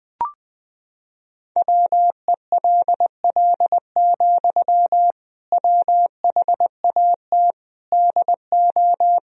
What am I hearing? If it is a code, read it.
Morse code, heard as a keyed tone, about 20 wpm, 700 Hz. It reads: WELL, WHAT DO
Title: Sentences from Top 200 Words and Abbreviations 20wpm